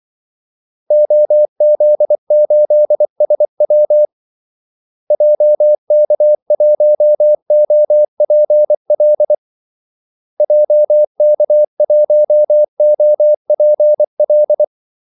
Morse code, heard as a keyed tone, about 24 wpm, 600 Hz. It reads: OZ8SW JK1OPL JK1OPL
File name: Callsigns-24wpm